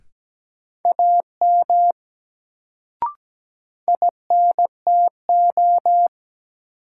Morse code, heard as a keyed tone, about 17 wpm, 700 Hz. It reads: AM INTO
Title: Sets of 2 Words - New Word Drill - 17wpm